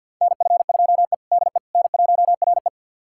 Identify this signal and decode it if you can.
Morse code, heard as a keyed tone, about 53 wpm, 700 Hz. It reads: NF2E DE N1RE
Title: USA Callsign DE Callsign - 4 Characters - 50wpm